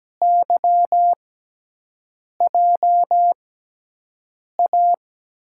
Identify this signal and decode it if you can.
Morse code, heard as a keyed tone, about 17 wpm, spaced as for 6 wpm, 700 Hz. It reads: YJA
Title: USA Call Signs - 6x Character Spacing - 17wpm